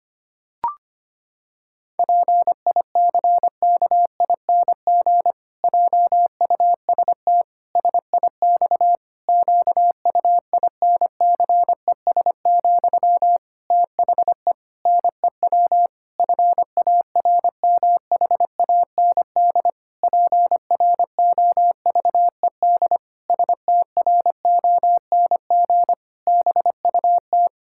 Morse code, heard as a keyed tone, about 25 wpm, 700 Hz. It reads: PICKING JUST SIX QUINCES, THE NEW FARMHAND PROVED STRONG BUT
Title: Pangram Challenge v2 - 25wpm